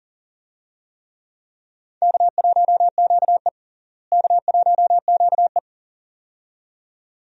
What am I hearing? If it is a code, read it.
Morse code, heard as a keyed tone, about 40 wpm, 700 Hz. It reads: K1QE K1QE
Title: USA Call Signs - 4 Characters - Repeated 2x - 40wpm